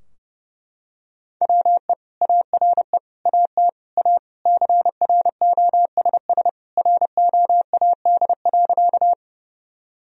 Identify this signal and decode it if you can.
Morse code, heard as a keyed tone, about 30 wpm, 700 Hz. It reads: WE ARE AT A CROSS ROAD.
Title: Sentences from Top 1000 Words - Review Part 2 - 30wpm